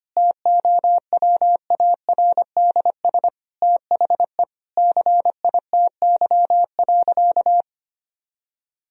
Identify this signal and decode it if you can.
Morse code, heard as a keyed tone, about 25 wpm, 700 Hz. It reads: TOWARDS THE CITY.
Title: Sentences from Top 1000 Words - Review Part 2 - 25wpm